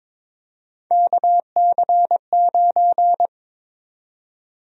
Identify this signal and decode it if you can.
Morse code, heard as a keyed tone, about 22 wpm, 700 Hz. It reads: KC9
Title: USA Call Sign Prefixes - Repeated 2x - 22wpm